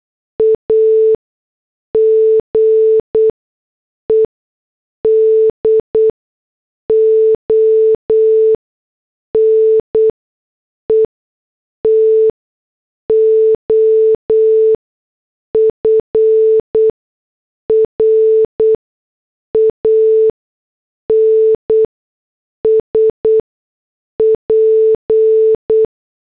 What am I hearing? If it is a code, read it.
Morse code, heard as a keyed tone, about 8 wpm, spaced as for 6 wpm, 440 Hz. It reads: AGED ONE TO FRANSP
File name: morse / headlines